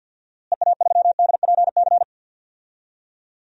Morse code, heard as a keyed tone, about 52 wpm, 700 Hz. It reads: EA3DPC
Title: DXCC-100 Call Signs - 6 Characters - Repeated 3x - 50wpm